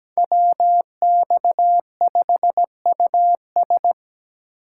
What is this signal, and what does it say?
Morse code, heard as a keyed tone, about 20 wpm, 700 Hz. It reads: WX5US
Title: Speed Racing - USA 2x2 Call Signs 17wpm